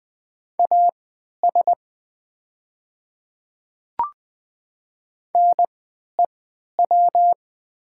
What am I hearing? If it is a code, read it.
Morse code, heard as a keyed tone, about 20 wpm, spaced as for 11 wpm, 700 Hz. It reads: AS NEW
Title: 3x Character Spacing - Top 300 Words - 20wpm